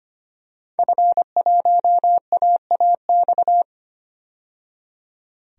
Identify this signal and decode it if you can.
Morse code, heard as a keyed tone, about 25 wpm, 700 Hz. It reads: F1AAX
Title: DXCC-100 Call Signs - 5 Characters - 25wpm